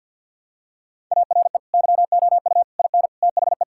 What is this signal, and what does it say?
Morse code, heard as a keyed tone, about 51 wpm, 700 Hz. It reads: ARE YOU IN THE
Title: Sentences from Top 1000 Words - Review Part 6 - 50wpm